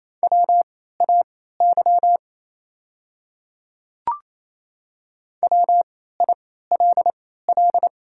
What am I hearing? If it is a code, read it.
Morse code, heard as a keyed tone, about 27 wpm, spaced as for 16 wpm, 690 Hz. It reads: WAY WILL
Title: 3x Character Spacing - Top 100 Words - 28wpm